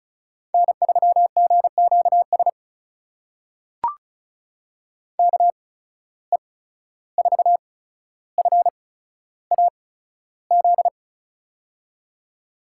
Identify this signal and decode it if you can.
Morse code, heard as a keyed tone, about 36 wpm, spaced as for 14 wpm, 700 Hz. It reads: N3GQS KE4FAZ
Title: USA Call Signs - 8x Character Spacing - 35wpm